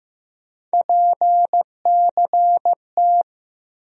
Morse code, heard as a keyed tone, about 15 wpm, 700 Hz. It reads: PCT
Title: USA Call Sign Postfixes - 3 Letters 15wpm